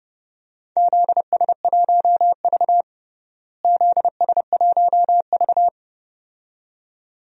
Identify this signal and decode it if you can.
Morse code, heard as a keyed tone, about 30 wpm, 700 Hz. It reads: ZS1V ZS1V
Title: DXCC-100 Call Signs - 4 Characters - Repeated 2x - 30wpm